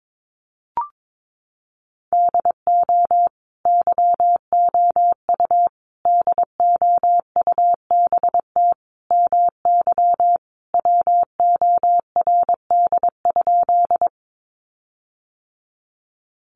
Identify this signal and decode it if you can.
Morse code, heard as a keyed tone, about 22 wpm, 700 Hz. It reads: DO YOU DOUBT MY WORD?
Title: Sentences from Top 1500 Words 22wpm